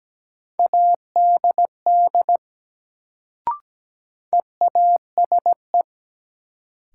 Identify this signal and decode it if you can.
Morse code, heard as a keyed tone, about 17 wpm, 700 Hz. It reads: ADD EASE
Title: Sentences from Top 1000 Words - Review Part 1 - New Word Drill - 17wpm